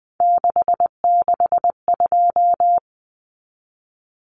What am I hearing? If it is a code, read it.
Morse code, heard as a keyed tone, about 20 wpm, 700 Hz. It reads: 662
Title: Common Cut Numbers - 3-Digit 20wpm